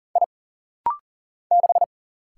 Morse code, heard as a keyed tone, about 40 wpm, 700 Hz. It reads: I 6
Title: Single Letter-Number - Rapid-Fire 40wpm